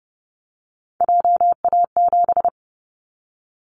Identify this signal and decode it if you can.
Morse code, heard as a keyed tone, about 32 wpm, 700 Hz. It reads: JA7
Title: Callsign Prefixes for Top 50 DXCC - 30wpm